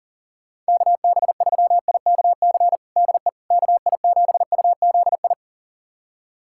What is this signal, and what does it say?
Morse code, heard as a keyed tone, about 40 wpm, 700 Hz. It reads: KB3IKC DE KI7UZI
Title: USA Callsign DE Callsign - 6 Characters - 40wpm